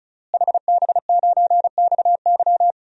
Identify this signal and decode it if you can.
Morse code, heard as a keyed tone, about 34 wpm, 700 Hz. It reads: HB9XY
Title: DXCC-100 Call Signs - 5 Characters - Repeated 3x - 35wpm